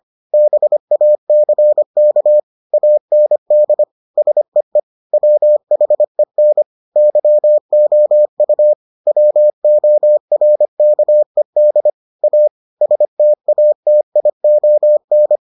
Morse code, heard as a keyed tone, about 25 wpm, 600 Hz. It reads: BACK AND SEE WHEN YOU WORKED A STATION